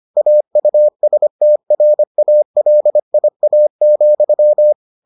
Morse code, heard as a keyed tone, about 25 wpm, 600 Hz. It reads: AUSTRALIA,